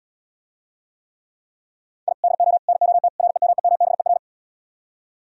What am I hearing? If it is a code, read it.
Morse code, heard as a keyed tone, about 77 wpm, 700 Hz. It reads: I DO NOT REMEMBER